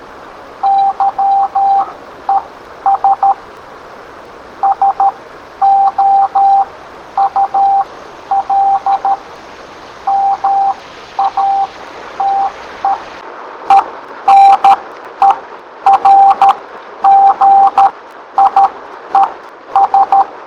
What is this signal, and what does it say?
Morse code, heard as a keyed tone, about 13 wpm, spaced as for 10 wpm, 790 Hz. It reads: YES SOUL MATE ENERGIES